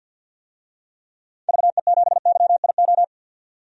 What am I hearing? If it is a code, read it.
Morse code, heard as a keyed tone, about 50 wpm, 700 Hz. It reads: VE7YIO
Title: DXCC-100 Call Signs - 6 Characters - 50wpm